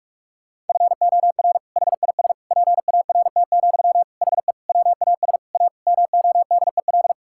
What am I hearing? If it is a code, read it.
Morse code, heard as a keyed tone, about 46 wpm, 700 Hz. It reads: FOR HIS PART, HE WAS A MODEL
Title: Sentences from Top 1000 Words - Review Part 5 - 45wpm